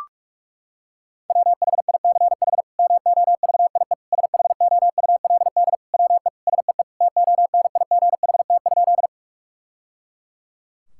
Morse code, heard as a keyed tone, about 42 wpm, 700 Hz. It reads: WHICH MOVIE SHOULD WE SEE TONIGHT?
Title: Sentences from Top 1000 Words - Review Part 5 - 45wpm